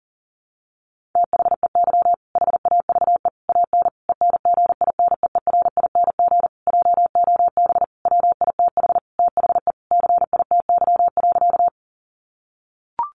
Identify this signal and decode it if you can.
Morse code, heard as a keyed tone, about 37 wpm, 700 Hz. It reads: THEY HAVE AN ENGINEERING JOB WITH THE CITY.
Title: Sentences from Top 1500 Words 40wpm